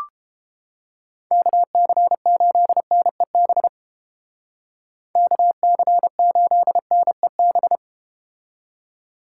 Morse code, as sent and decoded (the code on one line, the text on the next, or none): -.- -.-. ---.. -. -... / -.- -.-. ---.. -. -...
KC8NEB KC8NEB